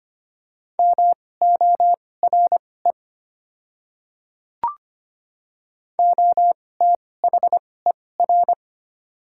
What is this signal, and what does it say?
Morse code, heard as a keyed tone, about 25 wpm, spaced as for 18 wpm, 700 Hz. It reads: MORE OTHER
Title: 2x Character Spacing - Top 100 Words - 25wpm